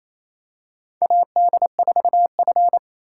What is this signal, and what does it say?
Morse code, heard as a keyed tone, about 28 wpm, 700 Hz. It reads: AD4F
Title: USA Call Signs - 4 Characters - 3x Character Spacing - 28wpm